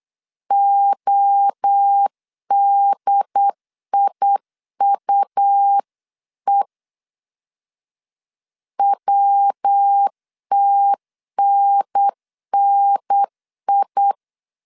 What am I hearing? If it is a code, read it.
Morse code, heard as a keyed tone, about 8 wpm, 800 Hz. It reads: ODIUE WTNNI